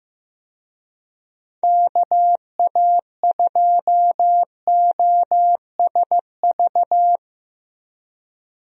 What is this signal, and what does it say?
Morse code, heard as a keyed tone, about 15 wpm, 700 Hz. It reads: KA2OSV
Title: USA Call Signs - 6 Characters - Repeated 2x - 15wpm